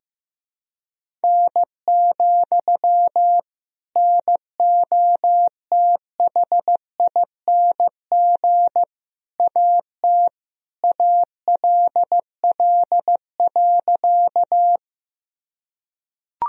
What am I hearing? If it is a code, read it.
Morse code, heard as a keyed tone, about 16 wpm, 700 Hz. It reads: N, NOTHING AT ALL.
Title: Sentences from Top 500 Words and Abbreviations 15wpm